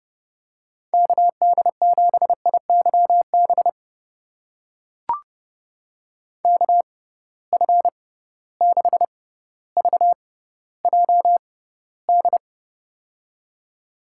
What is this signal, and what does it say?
Morse code, heard as a keyed tone, about 31 wpm, spaced as for 15 wpm, 700 Hz. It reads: KD7IYB KF6VJD
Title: USA Call Signs - 6 Characters - 6x Character Spacing - 30wpm